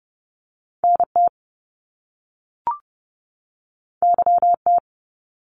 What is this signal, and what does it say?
Morse code, heard as a keyed tone, about 30 wpm, 700 Hz. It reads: NT YT
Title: Canadian Provinces and Territories 30wpm